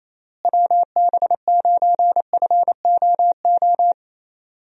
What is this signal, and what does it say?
Morse code, heard as a keyed tone, about 28 wpm, 700 Hz. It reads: WB9FOO
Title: USA Call Signs - 6 Characters - Repeated 3x - 28wpm